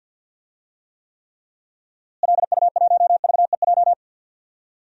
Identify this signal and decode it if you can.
Morse code, heard as a keyed tone, about 48 wpm, 700 Hz. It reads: LU1VEJ
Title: DXCC-100 Call Signs - 6 Characters - Repeated 3x - 50wpm